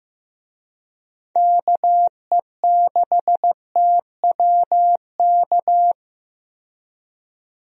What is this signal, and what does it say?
Morse code, heard as a keyed tone, about 15 wpm, 700 Hz. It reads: KE6TWK
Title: USA Call Signs - 6 Characters - 15wpm